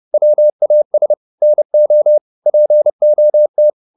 Morse code, heard as a keyed tone, about 30 wpm, 600 Hz. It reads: WAS NO POT
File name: the old man and the sea / the old man and the sea 1 30wpm